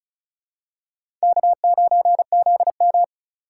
Morse code, heard as a keyed tone, about 36 wpm, 700 Hz. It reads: K9ZM
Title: USA Call Signs - 4 Characters - 4x Character Spacing - 35wpm